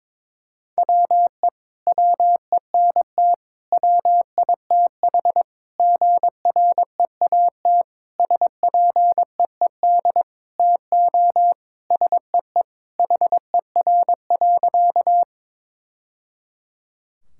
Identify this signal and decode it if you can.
Morse code, heard as a keyed tone, about 22 wpm, 700 Hz. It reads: WE WENT WITH GREAT SPEED TO SEE HER.
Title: Sentences from Top 1000 Words - Review Part 5 - 22wpm